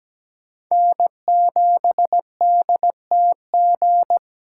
Morse code, heard as a keyed tone, about 20 wpm, 700 Hz. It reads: N7DTG
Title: Speed Racing - USA Call Signs 17wpm